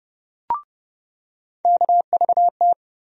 Morse code, heard as a keyed tone, about 30 wpm, 700 Hz. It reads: KVT
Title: USA Call Sign Postfixes - 3 Letters 30wpm